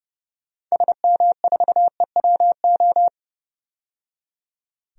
Morse code, heard as a keyed tone, about 30 wpm, 700 Hz. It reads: SM4EWO
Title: DXCC-100 Call Signs - 6 Characters - 30wpm